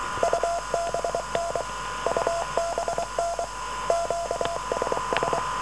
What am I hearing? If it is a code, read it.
Morse code, heard as a keyed tone, about 23 wpm, 680 Hz, with noise. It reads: UBN UBN QSS